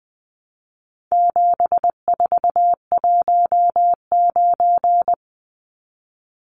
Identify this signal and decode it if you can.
Morse code, heard as a keyed tone, about 20 wpm, 700 Hz. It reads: 7419
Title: Numbers - 4-Digit 20wpm